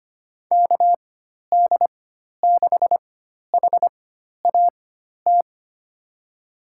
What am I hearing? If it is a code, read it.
Morse code, heard as a keyed tone, about 25 wpm, spaced as for 12 wpm, 700 Hz. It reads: KD6HAT
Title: USA Call Signs - 4x Character Spacing - 25wpm